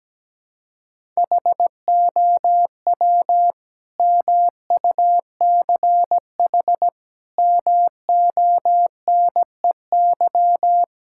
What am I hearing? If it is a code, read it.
Morse code, heard as a keyed tone, about 17 wpm, 700 Hz. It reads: HOW MUCH MONEY
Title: Sentences from Top 1000 Words - Review Part 5 - 17wpm